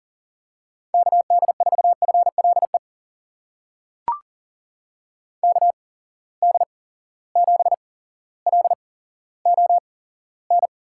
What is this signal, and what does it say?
Morse code, heard as a keyed tone, about 38 wpm, spaced as for 17 wpm, 700 Hz. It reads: KD4FLE KD7LON
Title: USA Call Signs - 6 Characters - 8x Character Spacing - 40wpm